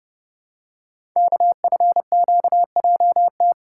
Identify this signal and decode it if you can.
Morse code, heard as a keyed tone, about 30 wpm, 700 Hz. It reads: KFQJT
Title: Code Groups - 5 Letters - 30wpm